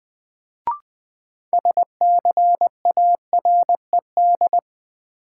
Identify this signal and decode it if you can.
Morse code, heard as a keyed tone, about 20 wpm, 700 Hz. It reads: SCARED
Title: Sentences from Top 700 Words - New Word Drill - 20wpm